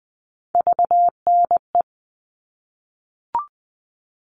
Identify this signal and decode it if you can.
Morse code, heard as a keyed tone, about 20 wpm, 700 Hz. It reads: VNE
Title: All Cut Numbers - 3-Digit 20wpm